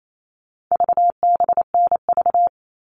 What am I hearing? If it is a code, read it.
Morse code, heard as a keyed tone, about 28 wpm, 700 Hz. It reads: VBNV
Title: All Cut Numbers - 4-Digit 28wpm